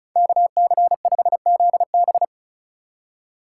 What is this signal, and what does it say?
Morse code, heard as a keyed tone, about 36 wpm, 700 Hz. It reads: KC5ZB
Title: USA Call Signs - 5 Characters - 4x Character Spacing - 35wpm